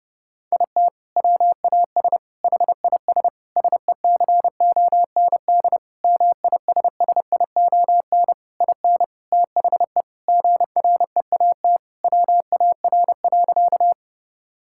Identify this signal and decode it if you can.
Morse code, heard as a keyed tone, about 32 wpm, 700 Hz. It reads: IT WAS HIS SECOND MISSION IN THE GREAT WAR.
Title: Sentences from Top 1600 Words 30wpm